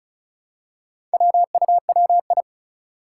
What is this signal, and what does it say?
Morse code, heard as a keyed tone, about 36 wpm, 700 Hz. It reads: WUWI
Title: Four Challenging Letter-Number Combinations 35wpm